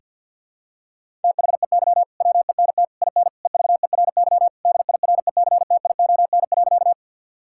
Morse code, heard as a keyed tone, about 53 wpm, 700 Hz. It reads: THEY WENT IN EVERY DIRECTION.
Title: Sentences from Top 1000 Words - Review Part 5 - 50wpm